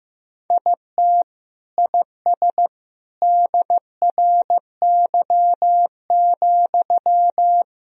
Morse code, heard as a keyed tone, about 15 wpm, 700 Hz. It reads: IT IS DRY,
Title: Sentences from Top 1000 Words - Review Part 3 - 15wpm